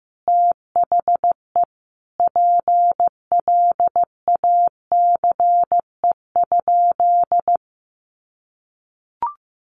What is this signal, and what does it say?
Morse code, heard as a keyed tone, about 15 wpm, 700 Hz. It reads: THE PLACE?
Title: Sentences from Top 200 Words 15wpm